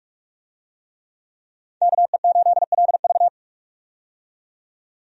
Morse code, heard as a keyed tone, about 43 wpm, 700 Hz. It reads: KE8LV
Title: USA Call Signs - 5 Characters - 45wpm